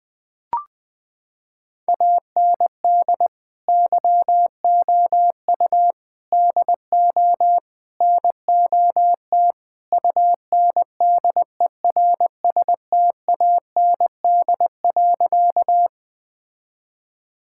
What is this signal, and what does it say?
Morse code, heard as a keyed tone, about 19 wpm, 700 Hz. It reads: AND YOU DO NOT UNDERSTAND.
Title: Sentences with 4-Word Phrases - 20wpm